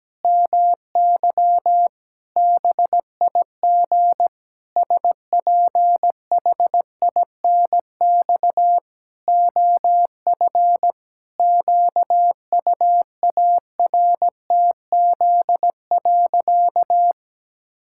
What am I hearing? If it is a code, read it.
Morse code, heard as a keyed tone, about 17 wpm, 700 Hz. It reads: MY BIG SPHINX OF QUARTZ.
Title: Pangram Challenge v2 - 17wpm